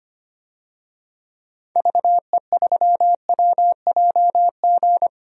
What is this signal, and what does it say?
Morse code, heard as a keyed tone, about 25 wpm, 700 Hz. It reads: VE3WJG
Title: DXCC-100 Call Signs - 6 Characters - Repeated 2x - 25wpm